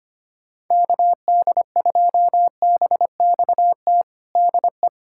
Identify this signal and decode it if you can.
Morse code, heard as a keyed tone, about 25 wpm, 700 Hz. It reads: KD2BXT DE
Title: USA Callsign DE Callsign 25wpm